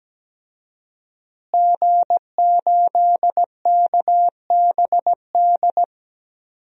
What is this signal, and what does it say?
Morse code, heard as a keyed tone, about 17 wpm, 700 Hz. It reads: G8KBD
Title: DXCC-100 Call Signs - 5 Characters - Repeated 2x - 17wpm